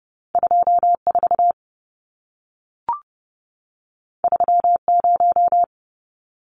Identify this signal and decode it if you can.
Morse code, heard as a keyed tone, about 30 wpm, 700 Hz. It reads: 24 30
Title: Numbers - 2-Digit 30wpm